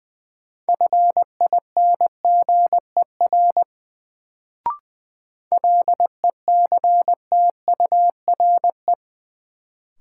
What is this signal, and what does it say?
Morse code, heard as a keyed tone, about 20 wpm, 700 Hz. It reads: FINGER LECTURE
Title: Sentences from Top 1600 Words - New Word Drill Alt - 20wpm